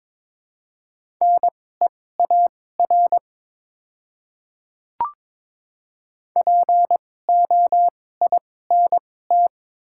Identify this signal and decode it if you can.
Morse code, heard as a keyed tone, about 22 wpm, spaced as for 16 wpm, 700 Hz. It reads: NEAR POINT
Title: 2x Character Spacing - Top 200 Words - 22wpm